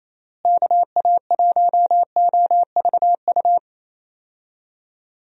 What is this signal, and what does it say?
Morse code, heard as a keyed tone, about 28 wpm, 700 Hz. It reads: KA1OVU
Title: USA Call Signs - 6 Characters - Repeated 3x - 28wpm